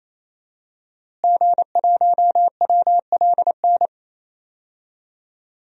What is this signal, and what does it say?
Morse code, heard as a keyed tone, about 28 wpm, 700 Hz. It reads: G1WLN
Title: DXCC-100 Call Signs - 5 Characters - Repeated 3x - 28wpm